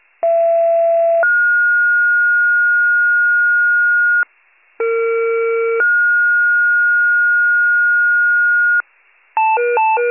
Two-tone page, heard as a one-second tone, then a three-second tone, about 670 then 1430 Hz, about 460 then 1430 Hz. It is heard from 0.2 to 8.8 s.